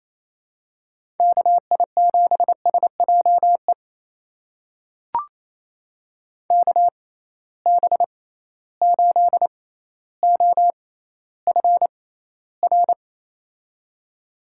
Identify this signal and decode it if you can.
Morse code, heard as a keyed tone, about 28 wpm, spaced as for 14 wpm, 700 Hz. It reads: KI7SJE KB8OFR
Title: USA Call Signs - 6x Character Spacing - 28wpm